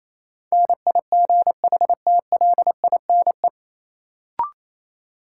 Morse code, heard as a keyed tone, about 28 wpm, 700 Hz. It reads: NIGHTLINE
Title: Sets of 5 Words - New Word Drill - 28wpm